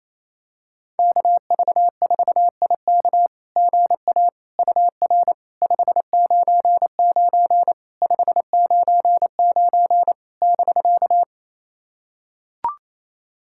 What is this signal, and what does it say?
Morse code, heard as a keyed tone, about 28 wpm, 690 Hz. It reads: KV4IK GA UR 599 599 <BK>
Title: SOTA QSO Practice 28wpm